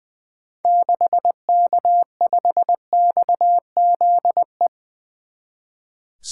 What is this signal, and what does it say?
Morse code, heard as a keyed tone, about 20 wpm, 700 Hz. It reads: 6K5XZE
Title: 2023 SOTA Activator Call Signs - 20wpm